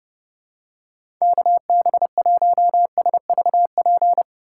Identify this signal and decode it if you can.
Morse code, heard as a keyed tone, about 29 wpm, 700 Hz. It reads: KB1SVP
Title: USA Call Signs - 6 Characters - 2x Character Spacing - 30wpm